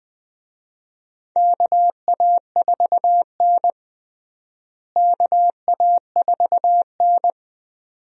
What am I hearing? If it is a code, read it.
Morse code, heard as a keyed tone, about 20 wpm, 700 Hz. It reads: KA4N KA4N
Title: USA Call Signs - 4 Characters - Repeated 2x - 20wpm